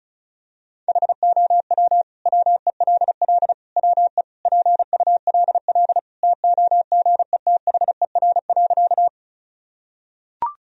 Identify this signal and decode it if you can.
Morse code, heard as a keyed tone, about 36 wpm, 700 Hz. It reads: HOW WELL WE PULL TOGETHER.
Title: Sentences from Top 1000 Words - Review Part 2 - 35wpm